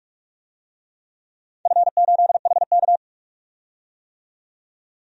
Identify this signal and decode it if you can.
Morse code, heard as a keyed tone, about 44 wpm, 700 Hz. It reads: F8HK
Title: DXCC-100 Call Signs - 4 Characters - 45wpm